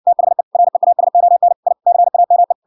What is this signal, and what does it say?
Morse code, heard as a keyed tone, about 59 wpm, 700 Hz. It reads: THE REASON I CAME